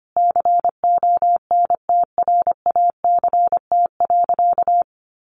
Morse code, heard as a keyed tone, about 25 wpm, 700 Hz. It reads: CONTRACT.
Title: Sentences from Top 1400 Words 25wpm